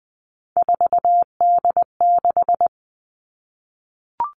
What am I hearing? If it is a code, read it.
Morse code, heard as a keyed tone, about 20 wpm, 700 Hz. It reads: 4D6
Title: All Cut Numbers - 3-Digit 20wpm